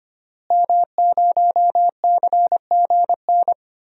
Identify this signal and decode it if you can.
Morse code, heard as a keyed tone, about 25 wpm, 700 Hz. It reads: M0CGN
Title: DXCC-100 Call Signs - 5 Characters - Repeated 3x - 25wpm